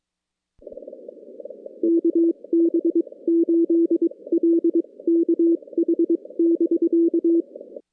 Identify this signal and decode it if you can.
Morse code, heard as a keyed tone, about 23 wpm, spaced as for 19 wpm, 340 Hz. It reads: KB8LKH <BK>
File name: contests / CW-SS-k5tr- -2100